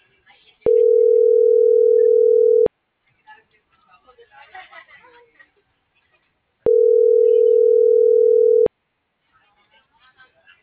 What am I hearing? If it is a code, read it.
Ringback tone (North America), 440 + 480 Hz.